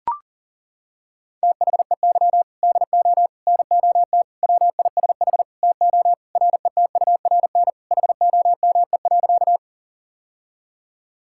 Morse code, heard as a keyed tone, about 40 wpm, 700 Hz. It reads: THEY DO NOT WISH TO RETURN HOME.
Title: Sentences from Top 1600 Words 40wpm